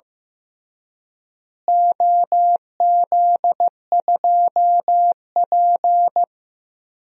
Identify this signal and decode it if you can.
Morse code, heard as a keyed tone, about 15 wpm, 700 Hz. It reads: OZ2P
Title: DXCC-100 Call Signs - 4 Characters - Repeated 3x - 15wpm